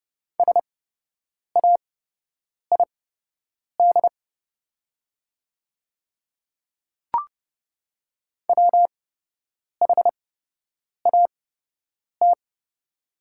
Morse code, heard as a keyed tone, about 30 wpm, spaced as for 8 wpm, 700 Hz. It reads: SAID WHAT
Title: 8x Character Spacing - Top 300 Words - 30wpm